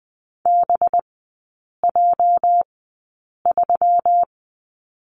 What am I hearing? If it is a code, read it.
Morse code, heard as a keyed tone, about 20 wpm, spaced as for 8 wpm, 700 Hz. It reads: BJ3
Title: 218 Lesson 35e - Number 0 - Three Characters 20wpm